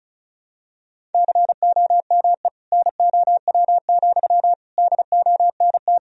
Morse code, heard as a keyed tone, about 34 wpm, 700 Hz. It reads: COME NOW, DONT
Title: Sentences from Top 1000 Words - Review Part 2 - 35wpm